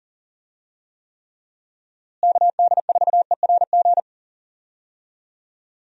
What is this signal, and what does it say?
Morse code, heard as a keyed tone, about 40 wpm, 700 Hz. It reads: KD4ERG
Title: USA Call Signs - 6 Characters - Repeated 2x - 40wpm